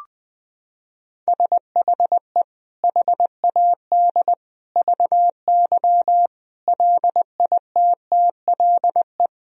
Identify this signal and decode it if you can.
Morse code, heard as a keyed tone, about 21 wpm, 700 Hz. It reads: SHE HAD VY LITTLE
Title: Sentences from Top 400 Words and Abbreviations 20wpm